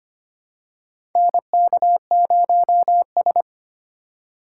Morse code, heard as a keyed tone, about 25 wpm, 700 Hz. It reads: NK0S
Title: USA Call Signs - 4x Character Spacing - 25wpm